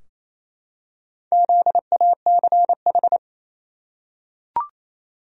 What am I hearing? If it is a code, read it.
Morse code, heard as a keyed tone, about 27 wpm, 700 Hz. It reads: ZACH
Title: USA Common Names - Max 4 Letters - 28wpm